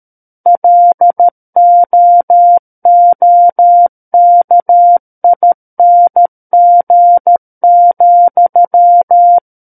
Morse code, heard as a keyed tone, about 13 wpm, 700 Hz. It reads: LOOKING,
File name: Book One / 04-Chapter 4 - 13 WPM